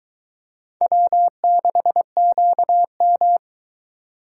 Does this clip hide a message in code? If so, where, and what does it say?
Yes: Morse code from 0.8 to 3.4 s, heard as a keyed tone, about 22 wpm, 700 Hz. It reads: W6QM